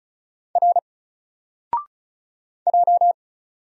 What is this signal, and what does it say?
Morse code, heard as a keyed tone, about 35 wpm, 700 Hz. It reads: R J
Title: Single Letter-Number Alt 35wpm